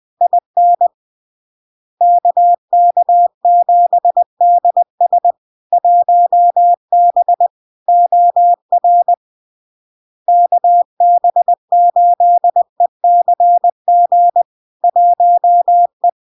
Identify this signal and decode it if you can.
Morse code, heard as a keyed tone, about 20 wpm, 700 Hz. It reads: IN KK7DS 1B OR KB8ECG 1E
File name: LARK-FD-2022-Log-in-CW-20wpm-700-Hz